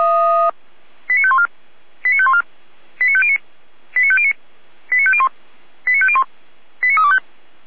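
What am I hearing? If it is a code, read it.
Five-tone selective call (ZVEI), heard as quick runs of five tones, tones of about 70 ms. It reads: 89514 89514 89509 89509 89501 89501 89325
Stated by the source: ZVEI five-tone selective call 89230, received on an SDR.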